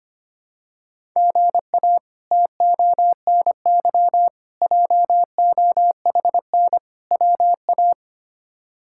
Morse code, heard as a keyed tone, about 25 wpm, 700 Hz. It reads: GA TONY JOHN WA
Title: K1USN SST QSO Practice - 25wpm